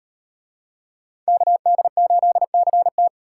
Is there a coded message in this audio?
Yes: Morse code from 1.3 to 3.1 s, heard as a keyed tone, about 37 wpm, 700 Hz. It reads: KD8CT